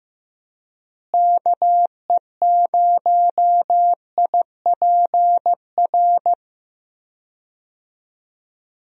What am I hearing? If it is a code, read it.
Morse code, heard as a keyed tone, about 15 wpm, 700 Hz. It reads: KE0IPR
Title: USA Call Signs - 6 Characters - Repeated 3x - 15wpm